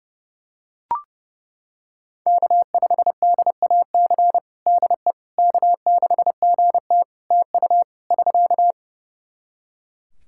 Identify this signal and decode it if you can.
Morse code, heard as a keyed tone, about 30 wpm, 700 Hz. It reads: K5DAC DE K6GT TU <SK>
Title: Practice QSO - Exchange 4 - Ending - 30wpm